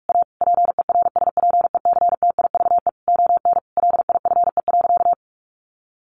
Morse code, heard as a keyed tone, about 44 wpm, 700 Hz. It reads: A PERSPECTIVE ON LIFE.
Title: Sentences from Top 1200 Words 45wpm